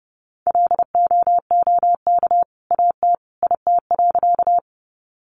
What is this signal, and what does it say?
Morse code, heard as a keyed tone, about 30 wpm, 700 Hz. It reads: LOOK AT IT.
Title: Sentences from Top 100 Words 30wpm